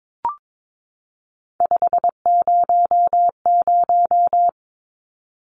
Morse code, heard as a keyed tone, about 22 wpm, 700 Hz. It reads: 500
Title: Numbers - 3-Digit 22wpm